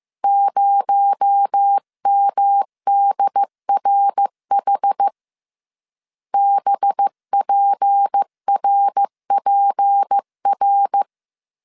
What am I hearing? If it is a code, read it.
Morse code, heard as a keyed tone, about 15 wpm, 800 Hz. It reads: 0MDRH BPRPR
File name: CW / DET / 1 6 1500C S C 14wpm 2951